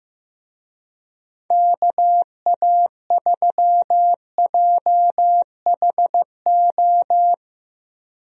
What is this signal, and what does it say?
Morse code, heard as a keyed tone, about 15 wpm, 700 Hz. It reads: KA3JHO
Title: USA Call Signs - 6 Characters - 15wpm